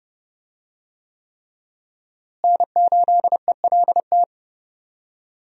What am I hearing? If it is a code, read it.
Morse code, heard as a keyed tone, about 31 wpm, 700 Hz. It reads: N8ELT
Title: USA Call Signs - 5 Characters - Repeated 3x - 30wpm